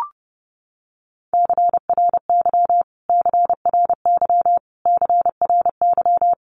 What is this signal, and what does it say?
Morse code, heard as a keyed tone, about 30 wpm, 700 Hz. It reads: CRY CRY CRY
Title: Top 500 Words Repeated 30wpm